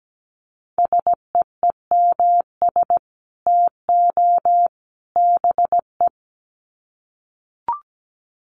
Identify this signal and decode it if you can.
Morse code, heard as a keyed tone, about 17 wpm, 700 Hz. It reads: SEEMS TO BE
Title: Sets of 3 Words 17wpm